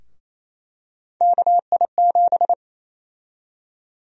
Morse code, heard as a keyed tone, about 27 wpm, 700 Hz. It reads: KI7
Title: USA Call Sign Prefixes - Repeated 2x - 28wpm